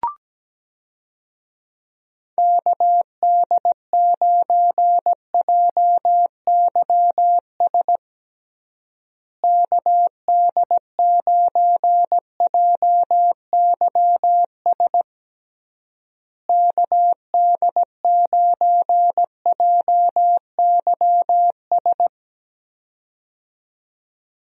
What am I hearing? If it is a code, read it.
Morse code, heard as a keyed tone, about 17 wpm, 700 Hz. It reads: KD9JYS KD9JYS KD9JYS
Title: USA Call Signs - 6 Characters - Repeated 3x - 17wpm